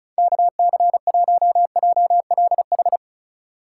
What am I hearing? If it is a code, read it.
Morse code, heard as a keyed tone, about 36 wpm, 700 Hz. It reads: KC1JLH
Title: USA Call Signs - 6 Characters - 4x Character Spacing - 35wpm